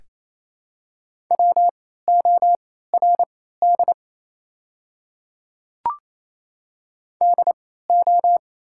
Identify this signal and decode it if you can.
Morse code, heard as a keyed tone, about 28 wpm, spaced as for 16 wpm, 700 Hz. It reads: WORD DO
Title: 3x Character Spacing - Top 100 Words - 28wpm